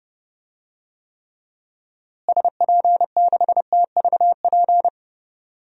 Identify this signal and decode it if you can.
Morse code, heard as a keyed tone, about 30 wpm, 700 Hz. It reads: SP6TVP
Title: DXCC-100 Call Signs - 6 Characters - Repeated 2x - 30wpm